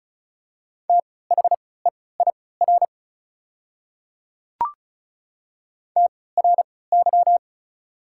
Morse code, heard as a keyed tone, about 34 wpm, spaced as for 20 wpm, 700 Hz. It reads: THEIR TRY
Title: 3x Character Spacing - Top 200 Words - 35wpm